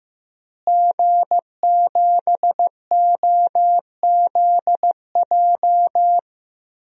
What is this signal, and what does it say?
Morse code, heard as a keyed tone, about 15 wpm, 700 Hz. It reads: G7OZJ
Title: DXCC-100 Call Signs - 5 Characters - Repeated 3x - 15wpm